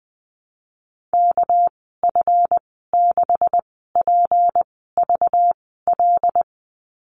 Morse code, heard as a keyed tone, about 19 wpm, spaced as for 14 wpm, 700 Hz. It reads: KF6PVL
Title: 213 Lesson 34g - Number 8 - Call Signs 20wpm